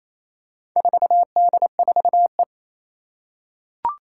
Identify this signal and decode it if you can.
Morse code, heard as a keyed tone, about 29 wpm, 700 Hz. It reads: 4D4E
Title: Four Challenging Letter-Number Combinations 28wpm